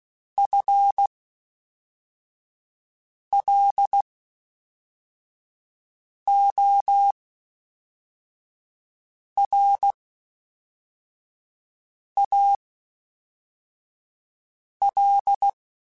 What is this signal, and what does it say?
Morse code, heard as a keyed tone, about 16 wpm, spaced as for 4 wpm, 800 Hz. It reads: FLORAL
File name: Spaced / FL Runs sp mp3s / FL Runs sp @5